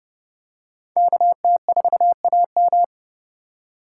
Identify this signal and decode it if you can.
Morse code, heard as a keyed tone, about 29 wpm, 700 Hz. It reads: KT4AM